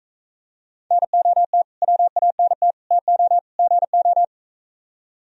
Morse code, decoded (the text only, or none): NOT WANT TO GO